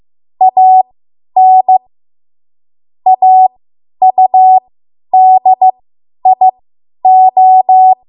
Morse code, heard as a keyed tone, about 15 wpm, spaced as for 10 wpm, 750 Hz. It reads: AN AUDIO
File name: Morse / 10WPM